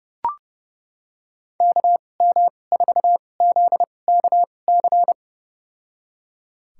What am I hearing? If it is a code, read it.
Morse code, heard as a keyed tone, about 30 wpm, spaced as for 22 wpm, 700 Hz. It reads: KM4ZKC
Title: USA Call Signs - 6 Characters - 2x Character Spacing - 30wpm